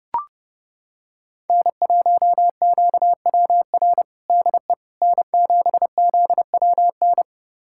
Morse code, heard as a keyed tone, about 30 wpm, 700 Hz. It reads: N1QWR DE N7ZWN
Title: USA Callsign DE Callsign - 5 Characters - 30wpm